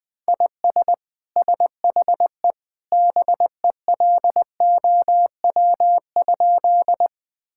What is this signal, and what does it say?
Morse code, heard as a keyed tone, about 21 wpm, 700 Hz. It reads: IS SHE BELOW?
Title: Sentences from Top 1000 Words - Review Part 5 - 20wpm